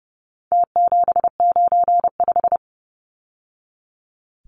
Morse code, heard as a keyed tone, about 30 wpm, 700 Hz. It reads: T795
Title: Numbers - Contest Style - 4-Digit 30wpm